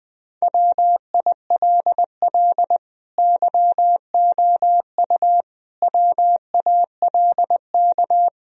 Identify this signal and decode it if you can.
Morse code, heard as a keyed tone, about 20 wpm, 700 Hz. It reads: WILL YOU WALK
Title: Sentences from Top 1000 Words - Review Part 2 - 20wpm